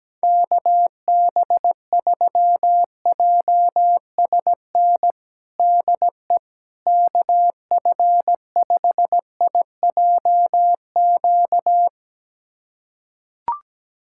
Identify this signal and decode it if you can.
Morse code, heard as a keyed tone, about 17 wpm, 700 Hz. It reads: KB3JSN DE KF5IJQ
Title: USA Callsign DE Callsign - 6 Characters - 17wpm